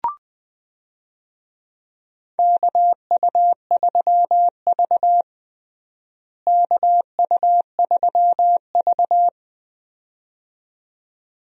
Morse code, heard as a keyed tone, about 20 wpm, 700 Hz. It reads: KU3V KU3V
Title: USA Call Signs - 4 Characters - Repeated 2x - 20wpm